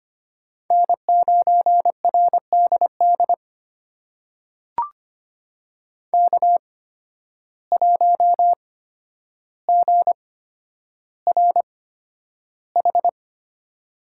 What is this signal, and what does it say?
Morse code, heard as a keyed tone, about 25 wpm, spaced as for 11 wpm, 700 Hz. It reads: N9RDD K1GRH
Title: USA Call Signs - 5 Characters - 8x Character Spacing - 25wpm